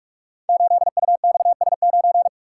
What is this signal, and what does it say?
Morse code, heard as a keyed tone, about 44 wpm, 700 Hz. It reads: 8UXS9
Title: Code Groups - 5 Letter:Numbers - 45wpm